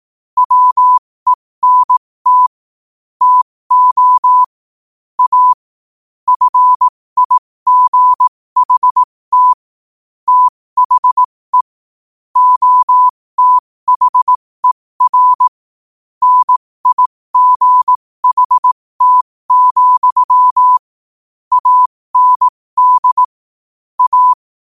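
Morse code, heard as a keyed tone, about 18 wpm, spaced as for 15 wpm, 1000 Hz. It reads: WENT TO A FIGHT THE OTHER NIGHT, AND A
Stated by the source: Morse practice sent at 15 wpm.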